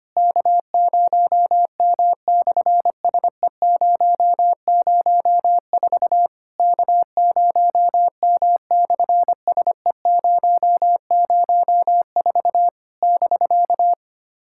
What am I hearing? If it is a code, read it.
Morse code, heard as a keyed tone, about 25 wpm, 700 Hz. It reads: K0M/SE004 K0M/SE004 <BK>
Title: SOTA QSO Practice 25wpm